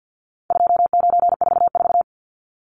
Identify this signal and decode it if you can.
Morse code, heard as a keyed tone, about 50 wpm, 700 Hz. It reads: 2944
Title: Numbers - 4-Digit 50wpm